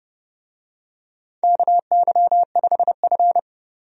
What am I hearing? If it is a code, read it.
Morse code, heard as a keyed tone, about 30 wpm, 700 Hz. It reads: KY5F
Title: USA Call Signs - 4 Characters - 30wpm